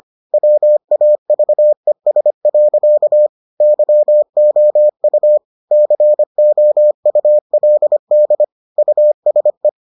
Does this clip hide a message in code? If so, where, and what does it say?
Yes: Morse code from 0.3 to 9.7 s, heard as a keyed tone, about 25 wpm, 600 Hz. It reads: WAVES. YOU COULD USE